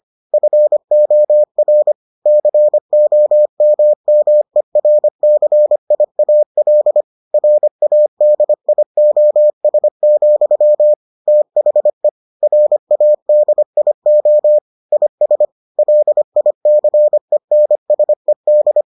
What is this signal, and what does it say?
Morse code, heard as a keyed tone, about 25 wpm, 600 Hz. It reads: FOR COMMERCIAL RADIOS, THE RADIO IS LICENSED